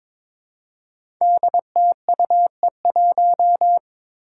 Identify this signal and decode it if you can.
Morse code, heard as a keyed tone, about 22 wpm, 700 Hz. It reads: DTUE1
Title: Five Challenging Letter-Number Combinations 22wpm